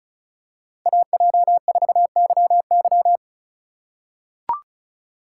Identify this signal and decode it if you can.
Morse code, heard as a keyed tone, about 36 wpm, 700 Hz. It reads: AJ4YY
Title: USA Call Signs - 5 Characters - 3x Character Spacing - 35wpm